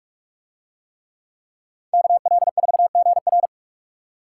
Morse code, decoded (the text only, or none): KL4GR